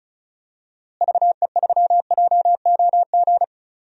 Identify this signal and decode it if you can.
Morse code, heard as a keyed tone, about 34 wpm, 700 Hz. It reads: VE3JOG
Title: DXCC-100 Call Signs - 6 Characters - Repeated 3x - 35wpm